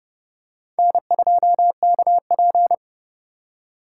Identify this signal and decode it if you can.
Morse code, heard as a keyed tone, about 29 wpm, 700 Hz. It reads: N2KP